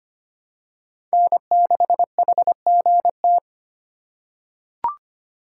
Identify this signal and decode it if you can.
Morse code, heard as a keyed tone, about 25 wpm, 700 Hz. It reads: N6HGT
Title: USA Call Signs - 5 Characters - 3x Character Spacing - 25wpm